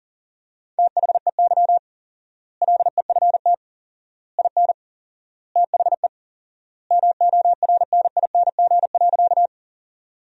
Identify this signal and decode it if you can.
Morse code, heard as a keyed tone, about 40 wpm, 700 Hz. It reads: THEY LEFT IN THE MORNING.
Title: 4x Word-Spacing - Sentences from Top 400 Words - 40wpm